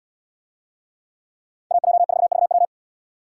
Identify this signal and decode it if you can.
Morse code, heard as a keyed tone, about 77 wpm, 700 Hz. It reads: AJ4VW